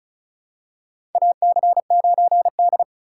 Morse code, heard as a keyed tone, about 36 wpm, 700 Hz. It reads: AC9D
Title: USA Call Signs - 4 Characters - 2x Character Spacing - 35wpm